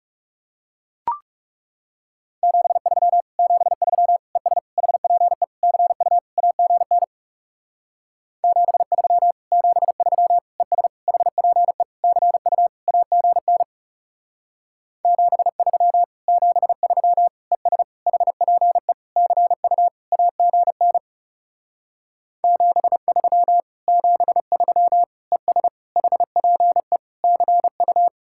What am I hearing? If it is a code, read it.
Morse code, heard as a keyed tone, about 37 wpm, 700 Hz. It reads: 73 73 ES HPE CU AGN 73 73 ES HPE CU AGN 73 73 ES HPE CU AGN 73 73 ES HPE CU
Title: Speed-Racing Common QSO Phrases Encore - 30wpm